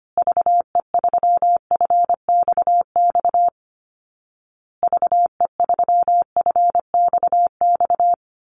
Morse code, heard as a keyed tone, about 25 wpm, 700 Hz. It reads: VE3FXX VE3FXX
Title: Call Signs Repeated 2x - 25wpm